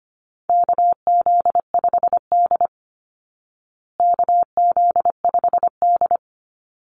Morse code, heard as a keyed tone, about 25 wpm, 700 Hz. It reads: KZ5D KZ5D
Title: Call Signs Repeated 2x - 25wpm